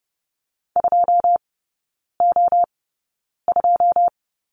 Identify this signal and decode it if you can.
Morse code, heard as a keyed tone, about 30 wpm, spaced as for 9 wpm, 700 Hz. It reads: 2O2
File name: 081 Lesson 14e - Number 2 - Three Characters 30wpm